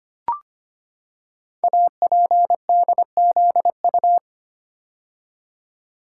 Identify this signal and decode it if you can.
Morse code, heard as a keyed tone, about 25 wpm, 700 Hz. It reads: APDZU
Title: Code Groups - 5 Letters - 25wpm